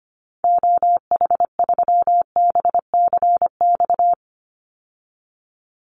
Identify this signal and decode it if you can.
Morse code, heard as a keyed tone, about 25 wpm, 700 Hz. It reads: OH3BCX
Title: Call Signs - 25wpm